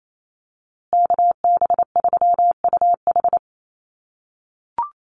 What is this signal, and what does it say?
Morse code, heard as a keyed tone, about 28 wpm, 700 Hz. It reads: KB3UH
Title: USA Call Signs Repeated 3x 28wpm